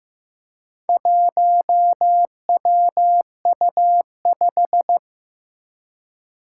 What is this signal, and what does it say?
Morse code, heard as a keyed tone, about 15 wpm, 700 Hz. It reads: 1WU5
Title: Four Challenging Letter-Number Combinations 15wpm